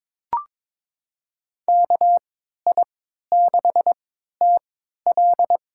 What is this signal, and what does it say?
Morse code, heard as a keyed tone, about 22 wpm, spaced as for 13 wpm, 700 Hz. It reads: KI6TL
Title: USA Call Signs - 5 Characters - 3x Character Spacing - 22wpm